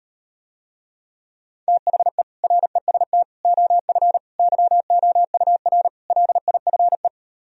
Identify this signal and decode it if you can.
Morse code, heard as a keyed tone, about 38 wpm, 700 Hz. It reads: THE REST OF YOUR LIFE